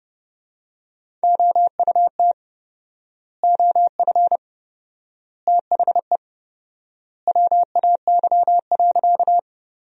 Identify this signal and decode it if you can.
Morse code, heard as a keyed tone, about 32 wpm, 700 Hz. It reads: OUT OF THE WAY.
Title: 4x Word-Spacing - Sentences from Top 100 Words - 30wpm